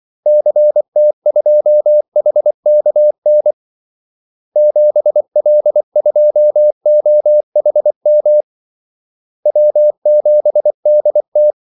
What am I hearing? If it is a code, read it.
Morse code, heard as a keyed tone, about 24 wpm, 600 Hz. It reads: CT2HKN 7L2OHM W7DT
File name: Callsigns-24wpm